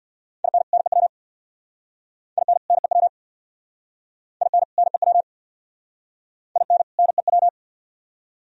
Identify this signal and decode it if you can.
Morse code, heard as a keyed tone, about 67 wpm, 700 Hz. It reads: IN NEW IN NEW IN NEW IN NEW